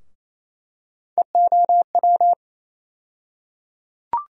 Morse code, heard as a keyed tone, about 28 wpm, 700 Hz. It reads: EOW
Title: USA Call Sign Postfixes - 3 Letters 28wpm